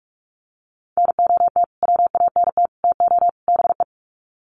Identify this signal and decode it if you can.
Morse code, heard as a keyed tone, about 44 wpm, 700 Hz. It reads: NOT WANT TO BE
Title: Sets of 4 Words 45wpm